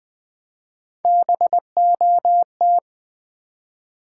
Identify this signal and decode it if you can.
Morse code, heard as a keyed tone, about 20 wpm, 700 Hz. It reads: BOT
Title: Build A Word - 1 Letter At A Time - 4-Letter Words - 20wpm